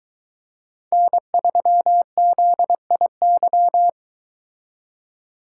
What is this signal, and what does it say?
Morse code, heard as a keyed tone, about 22 wpm, 700 Hz. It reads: N3ZIY